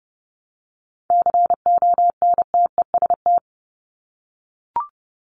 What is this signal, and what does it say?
Morse code, heard as a keyed tone, about 30 wpm, 700 Hz. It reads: CONTEST
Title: Top 1501-1750 Words 30wpm